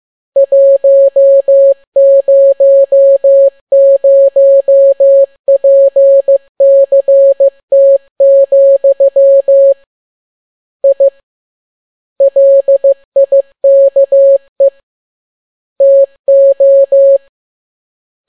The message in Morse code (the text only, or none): 100PCT, I LIKE TO